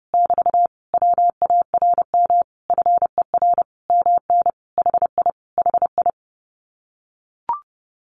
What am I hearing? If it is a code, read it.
Morse code, heard as a keyed tone, about 30 wpm, 700 Hz. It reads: <BT> WARM FER MN HI HI
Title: Common QSO Phrases Encore 30wpm